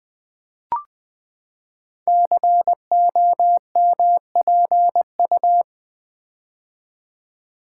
Morse code, heard as a keyed tone, about 20 wpm, 700 Hz. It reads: COMPU
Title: Build A Word - 5 Letters At A Time - 10-Letter Words - 20wpm